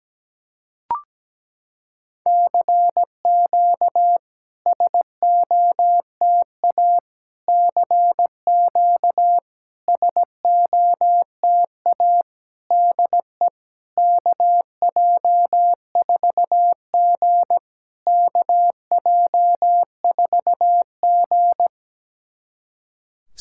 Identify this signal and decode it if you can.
Morse code, heard as a keyed tone, about 17 wpm, 700 Hz. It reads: CQ SOTA CQ SOTA DE KJ4G KJ4G
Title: SOTA QSO Practice 17wpm